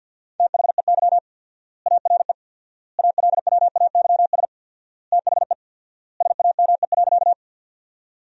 Morse code, heard as a keyed tone, about 53 wpm, 700 Hz. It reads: THEY ARE ALWAYS THE SAME.
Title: 4x Word-Spacing - Sentences from Top 300 Words - 50wpm